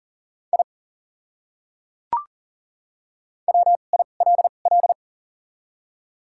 Morse code, heard as a keyed tone, about 40 wpm, spaced as for 29 wpm, 700 Hz. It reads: I WILL
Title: 2x Character Spacing - Top 200 Words - 40wpm